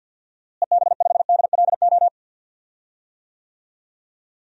Morse code, heard as a keyed tone, about 48 wpm, 700 Hz. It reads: EB5DLO
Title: DXCC-100 Call Signs - 6 Characters - Repeated 2x - 50wpm